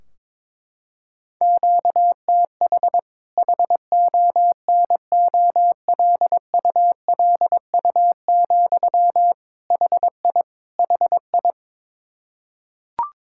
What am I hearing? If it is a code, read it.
Morse code, heard as a keyed tone, about 22 wpm, 700 Hz. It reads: QTH HONOLULU, HI HI
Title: Practice QSO - Exchange 1 - Beginning - 22wpm